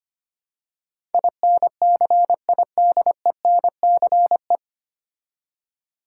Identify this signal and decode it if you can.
Morse code, heard as a keyed tone, about 25 wpm, 700 Hz. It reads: INCIDENCE
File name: practice / 25wpm / 9 Letter Words - First 3 Letters Given - 25wpm